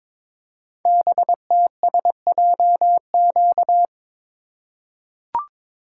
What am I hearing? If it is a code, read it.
Morse code, heard as a keyed tone, about 22 wpm, 700 Hz. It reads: BTSJQ
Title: Code Groups - 5 Letters - 22wpm